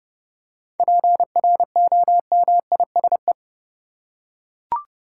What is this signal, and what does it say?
Morse code, heard as a keyed tone, about 30 wpm, 700 Hz. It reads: PROMISE
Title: Seven Letter Words - 30wpm